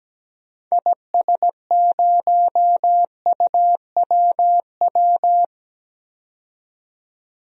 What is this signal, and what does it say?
Morse code, heard as a keyed tone, about 17 wpm, 700 Hz. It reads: IS0UWW
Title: DXCC-100 Call Signs - 6 Characters - 17wpm